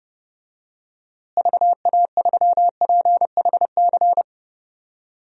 Morse code, heard as a keyed tone, about 32 wpm, 700 Hz. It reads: VA3PHC
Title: DXCC-100 Call Signs - 6 Characters - Repeated 3x - 30wpm